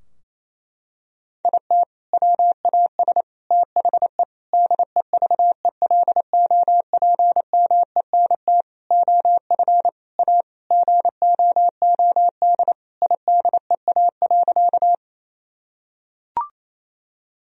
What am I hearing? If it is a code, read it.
Morse code, heard as a keyed tone, about 28 wpm, 700 Hz. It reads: IT WAS THE DEVELOPMENT OF A GOOD IDEA.
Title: Sentences from Top 1000 Words - Review Part 4 - 28wpm